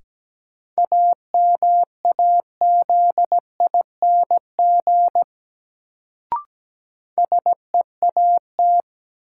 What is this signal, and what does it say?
Morse code, heard as a keyed tone, about 17 wpm, 700 Hz. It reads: AMAZING SEAT
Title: Sentences from Top 1500 Words - New Word Drill - 17wpm